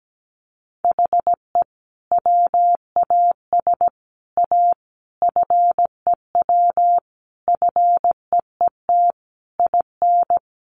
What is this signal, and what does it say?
Morse code, heard as a keyed tone, about 17 wpm, 700 Hz. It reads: HE WAS A FEW FEET IN
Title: Sentences from Top 1400 Words 17wpm